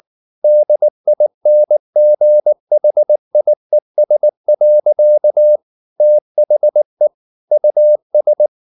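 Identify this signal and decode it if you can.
Morse code, heard as a keyed tone, about 19 wpm, 600 Hz. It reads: DINGHIES. THE US
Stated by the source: Morse code at 19 wpm from CW med Gnister, reading Ken Follett's novel Never in English.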